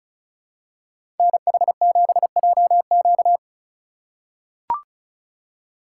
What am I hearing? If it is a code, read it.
Morse code, heard as a keyed tone, about 36 wpm, 700 Hz. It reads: NH7JQ
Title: USA Call Signs - 5 Characters - 3x Character Spacing - 35wpm